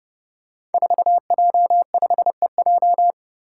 Morse code, heard as a keyed tone, about 29 wpm, 700 Hz. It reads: 4J5EJ
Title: Five Challenging Letter-Number Combinations 30wpm